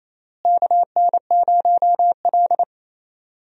Morse code, heard as a keyed tone, about 28 wpm, 700 Hz. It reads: KN0L
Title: USA Call Signs - 4 Characters - Repeated 3x - 28wpm